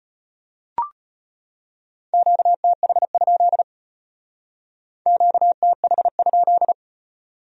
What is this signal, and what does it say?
Morse code, heard as a keyed tone, about 32 wpm, 700 Hz. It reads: QTH? QTH?